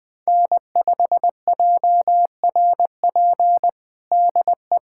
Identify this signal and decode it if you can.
Morse code, heard as a keyed tone, about 20 wpm, 700 Hz. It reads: N5JRP DE
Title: USA Callsign DE Callsign - 5 Characters - 20wpm